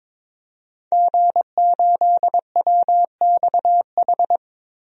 Morse code, heard as a keyed tone, about 22 wpm, 700 Hz. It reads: G8WXH
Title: DXCC-100 Call Signs - 5 Characters - Repeated 2x - 22wpm